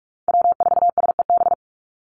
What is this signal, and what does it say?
Morse code, heard as a keyed tone, about 44 wpm, 700 Hz. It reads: W4SEB
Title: USA Call Signs 45wpm